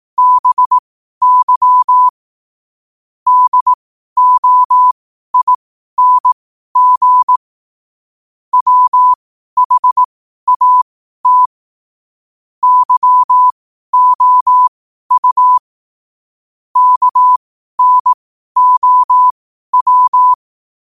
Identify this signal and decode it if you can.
Morse code, heard as a keyed tone, about 18 wpm, spaced as for 12 wpm, 1000 Hz. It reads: BY DOING WHAT YOU KNOW
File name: qotdfiles / 12wpm